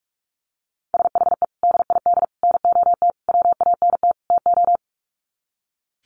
Morse code, heard as a keyed tone, about 44 wpm, 700 Hz. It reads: SHE DID NOT WANT TO
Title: Sets of 5 Words 45wpm